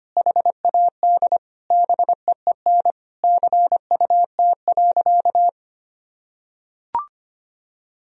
Morse code, heard as a keyed tone, about 25 wpm, 700 Hz. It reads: HAD BEEN CUT.
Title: Sentences from Top 1000 Words - Review Part 1 - 25wpm